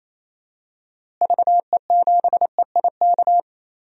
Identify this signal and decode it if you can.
Morse code, heard as a keyed tone, about 28 wpm, 700 Hz. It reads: VE7EIK
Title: DXCC-100 Call Signs - 6 Characters - Repeated 3x - 28wpm